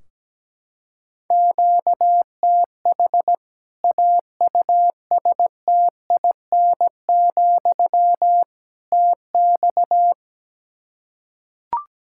Morse code, heard as a keyed tone, about 17 wpm, 700 Hz. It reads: QTH AUSTIN, TX
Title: Practice QSO - Exchange 1 - Beginning - 17wpm